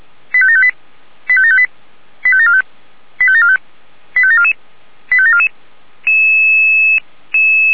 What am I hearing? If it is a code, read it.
Five-tone selective call (ZVEI), heard as quick runs of five tones, tones of about 70 ms. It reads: 85758 85758 85745 85745 85744 85744